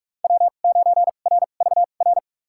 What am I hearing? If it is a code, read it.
Morse code, heard as a keyed tone, about 46 wpm, spaced as for 33 wpm, 700 Hz. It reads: W9RVR
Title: USA Call Signs - 2x Character Spacing - 45wpm